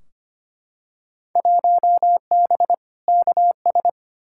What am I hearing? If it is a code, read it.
Morse code, heard as a keyed tone, about 25 wpm, 700 Hz. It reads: 1B KS
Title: Field Day Exchanges Encore - 25wpm